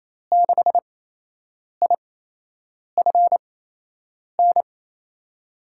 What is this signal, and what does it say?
Morse code, heard as a keyed tone, about 28 wpm, spaced as for 8 wpm, 700 Hz. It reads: 6IFN
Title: USA Call Signs - 5 Characters - 8x Character Spacing - 28wpm